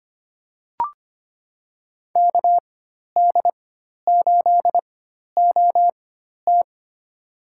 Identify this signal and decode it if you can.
Morse code, heard as a keyed tone, about 25 wpm, spaced as for 12 wpm, 700 Hz. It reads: KD8OT
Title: USA Call Signs - 5 Characters - 4x Character Spacing - 25wpm